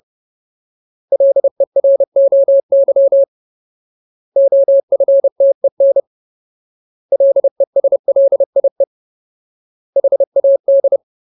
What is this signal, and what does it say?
Morse code, heard as a keyed tone, about 30 wpm, 550 Hz. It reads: LEROY OFTEN LESLIE HAD
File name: Ultimate Word List / 30WPM Ultimate